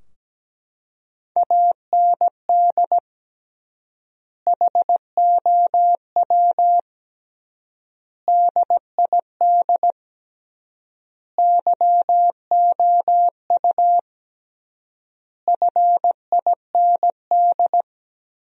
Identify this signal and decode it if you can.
Morse code, heard as a keyed tone, about 17 wpm, 700 Hz. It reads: AND HOW DID YOU FIND
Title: 3x Word-Spacing - Sentences from Top 200 Words - 17wpm